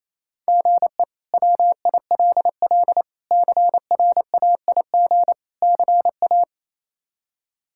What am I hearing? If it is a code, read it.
Morse code, heard as a keyed tone, about 28 wpm, 700 Hz. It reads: GE WILL CRAIG CA
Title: K1USN SST QSO Practice - 28wpm